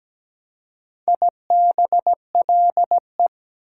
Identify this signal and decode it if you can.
Morse code, heard as a keyed tone, about 20 wpm, 700 Hz. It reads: IBLE
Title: Speed-Racing Prefixes and Suffixes 17wpm